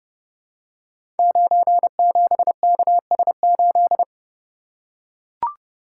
Morse code, heard as a keyed tone, about 30 wpm, 700 Hz. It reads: 97KS8
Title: Code Groups - 5 Letter:Numbers - 30wpm